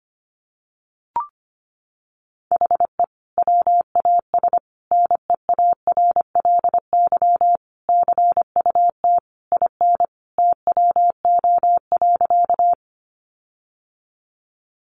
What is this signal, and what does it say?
Morse code, heard as a keyed tone, about 25 wpm, 700 Hz. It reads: HE WAS NEARLY CUT IN TWO.
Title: Sentences from Top 1300 Words 25wpm